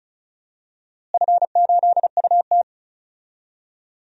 Morse code, heard as a keyed tone, about 36 wpm, 700 Hz. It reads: F8UT
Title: DXCC-100 Call Signs - 4 Characters - Repeated 3x - 35wpm